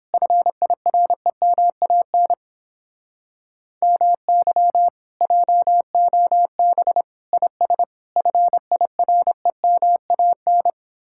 Morse code, heard as a keyed tone, about 29 wpm, 700 Hz. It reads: FIREMAN MY JOB IS FIREMAN